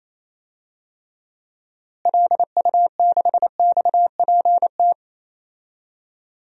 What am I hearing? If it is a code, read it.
Morse code, heard as a keyed tone, about 28 wpm, 700 Hz. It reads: LU6XPT
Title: DXCC-100 Call Signs - 6 Characters - Repeated 2x - 28wpm